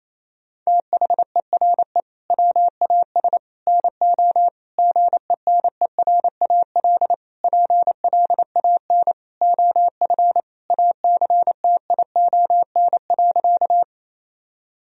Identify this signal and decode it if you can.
Morse code, heard as a keyed tone, about 28 wpm, 700 Hz. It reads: THERE WAS NO GENERAL PLAN OF ACTION.
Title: Sentences from Top 1000 Words - Review Part 4 - 28wpm